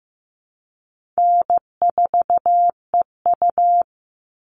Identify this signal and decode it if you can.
Morse code, heard as a keyed tone, about 15 wpm, 700 Hz. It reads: N4EU
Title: All Cut Numbers - 4-Digit 15wpm